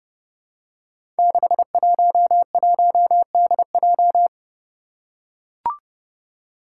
Morse code, heard as a keyed tone, about 30 wpm, 700 Hz. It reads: 611DJ
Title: Five Challenging Letter-Number Combinations 30wpm